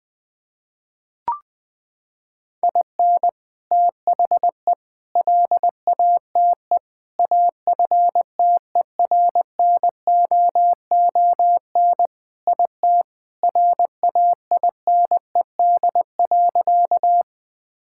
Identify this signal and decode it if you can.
Morse code, heard as a keyed tone, about 20 wpm, 700 Hz. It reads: IN THE LATE AFTERNOON IT RAINED.
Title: Sentences with 3-Word Phrases - 20wpm